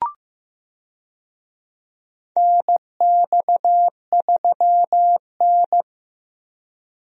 Morse code, heard as a keyed tone, about 15 wpm, 700 Hz. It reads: NX3N
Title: USA Call Signs - 4 Characters - Repeated 2x - 15wpm